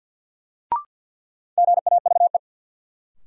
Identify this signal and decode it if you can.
Morse code, heard as a keyed tone, about 52 wpm, 700 Hz. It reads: GAVE
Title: Sentences from Top 500 Words - New Word Drill - 50wpm